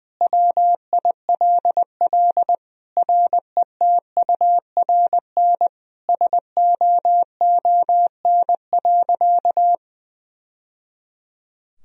Morse code, heard as a keyed tone, about 20 wpm, 700 Hz. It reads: WILL RETURN SOON.
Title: Sentences from Top 1000 Words - Review Part 2 - 20wpm